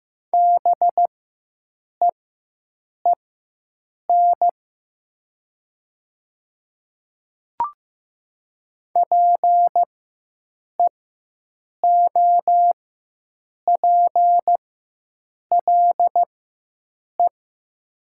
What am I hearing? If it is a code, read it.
Morse code, heard as a keyed tone, about 15 wpm, spaced as for 7 wpm, 700 Hz. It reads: BEEN PEOPLE
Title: 4x Character Spacing - Top 100 Words - 15wpm